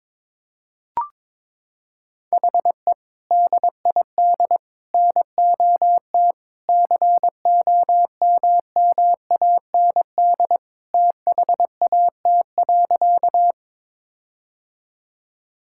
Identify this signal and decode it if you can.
Morse code, heard as a keyed tone, about 22 wpm, 700 Hz. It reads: HE DID NOT COMMAND THAT.
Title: Sentences from Top 1600 Words 22wpm